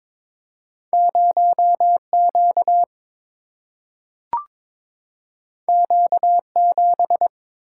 Morse code, heard as a keyed tone, about 22 wpm, 700 Hz. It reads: 0 Q Q 7
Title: Single-Letter Single-Number 22wpm